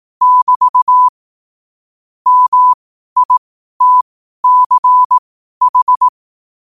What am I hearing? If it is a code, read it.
Morse code, heard as a keyed tone, about 18 wpm, spaced as for 12 wpm, 1000 Hz. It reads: <BT> MITCH
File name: qotdfiles / 12wpm